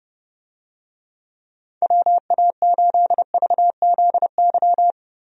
Morse code, heard as a keyed tone, about 31 wpm, 700 Hz. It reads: WA8VZY
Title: USA Call Signs - 6 Characters - Repeated 3x - 30wpm